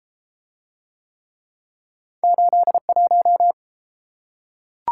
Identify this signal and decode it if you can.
Morse code, heard as a keyed tone, about 32 wpm, 700 Hz. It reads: 81